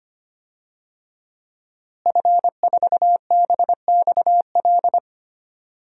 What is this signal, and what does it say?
Morse code, heard as a keyed tone, about 25 wpm, 700 Hz. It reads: F4BXL
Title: DXCC-100 Call Signs - 5 Characters - Repeated 3x - 25wpm